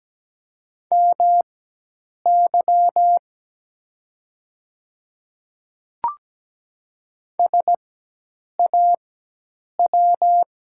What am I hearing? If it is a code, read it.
Morse code, heard as a keyed tone, about 17 wpm, spaced as for 8 wpm, 690 Hz. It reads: MY SAW
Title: 4x Character Spacing - Top 300 Words - 17wpm